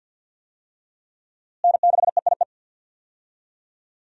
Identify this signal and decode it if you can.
Morse code, heard as a keyed tone, about 50 wpm, 700 Hz. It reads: N6EIE
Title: USA Call Signs - 5 Characters - Repeated 3x - 50wpm